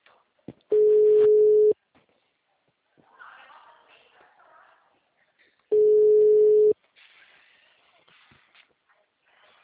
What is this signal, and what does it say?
Ringback tone (Europe, 425 Hz).
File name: out- -2006